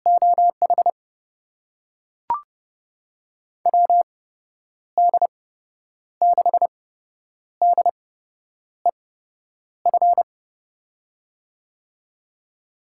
Morse code, heard as a keyed tone, about 31 wpm, spaced as for 9 wpm, 700 Hz. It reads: OH WD6DEF
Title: USA Call Signs - 6 Characters - 8x Character Spacing - 30wpm